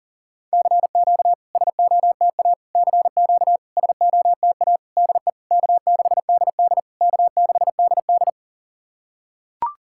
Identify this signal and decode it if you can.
Morse code, heard as a keyed tone, about 40 wpm, 690 Hz. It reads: CQ SOTA CQ SOTA DE K6DD K6DD
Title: SOTA QSO Practice 40wpm